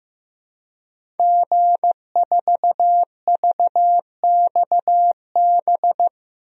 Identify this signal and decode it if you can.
Morse code, heard as a keyed tone, about 15 wpm, 700 Hz. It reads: G4VXB
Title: DXCC-100 Call Signs - 5 Characters - Repeated 2x - 15wpm